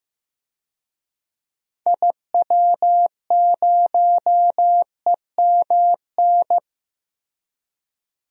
Morse code, heard as a keyed tone, about 15 wpm, 700 Hz. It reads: IW0EMN
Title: DXCC-100 Call Signs - 6 Characters - 15wpm